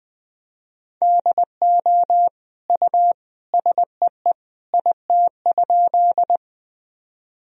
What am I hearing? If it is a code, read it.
Morse code, heard as a keyed tone, about 20 wpm, 700 Hz. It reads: DO U SEE IT?
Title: Sentences from Top 100 Words and Abbreviations 20wpm